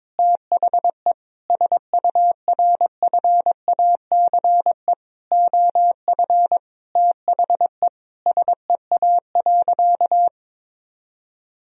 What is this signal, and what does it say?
Morse code, heard as a keyed tone, about 22 wpm, 700 Hz. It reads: THE SURFACE OF THE SEA.
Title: Sentences from Top 1000 Words - Review Part 3 - 22wpm